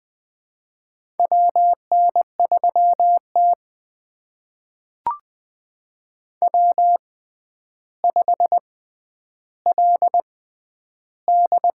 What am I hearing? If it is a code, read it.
Morse code, heard as a keyed tone, about 20 wpm, spaced as for 10 wpm, 700 Hz. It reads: WN3T W5LD
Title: USA Call Signs - 4 Characters - 6x Character Spacing - 20wpm